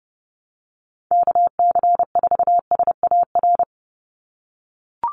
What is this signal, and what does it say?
Morse code, heard as a keyed tone, about 30 wpm, 700 Hz. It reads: KC4SAR
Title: USA Call Signs 30wpm